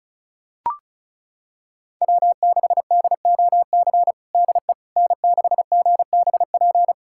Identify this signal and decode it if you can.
Morse code, heard as a keyed tone, about 36 wpm, 700 Hz. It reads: W6DOC DE N6GBP
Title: USA Callsign DE Callsign - 5 Characters - 35wpm